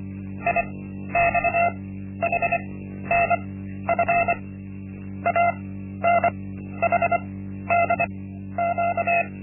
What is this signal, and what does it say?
Morse code, heard as a keyed tone, about 25 wpm, spaced as for 12 wpm, 690 Hz. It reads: IXHNF ANHDQ
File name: Linksmakalnis-1